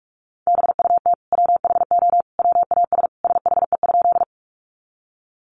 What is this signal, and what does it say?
Morse code, heard as a keyed tone, about 44 wpm, 700 Hz. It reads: BUT WHO WAS SHE?
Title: Sentences from Top 100 Words 45wpm